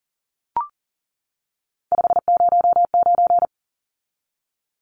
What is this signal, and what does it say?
Morse code, heard as a keyed tone, about 40 wpm, 700 Hz. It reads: 509
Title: Numbers - 3-Digit 40wpm